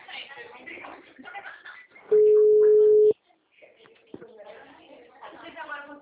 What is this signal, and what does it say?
Ringback tone (Europe, 425 Hz), with speech.